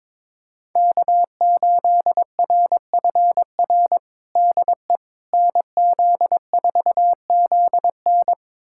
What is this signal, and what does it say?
Morse code, heard as a keyed tone, about 22 wpm, 700 Hz. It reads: K8RFR DE NZ4ZN
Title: USA Callsign DE Callsign - 5 Characters - 22wpm